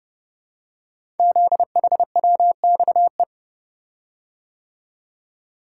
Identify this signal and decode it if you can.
Morse code, heard as a keyed tone, about 30 wpm, 700 Hz. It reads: ZHWXE
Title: Code Groups - 5 Letters - 30wpm